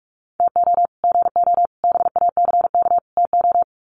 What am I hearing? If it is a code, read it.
Morse code, heard as a keyed tone, about 44 wpm, 700 Hz. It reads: TO GO BACK TO
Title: Sets of 4 Words 45wpm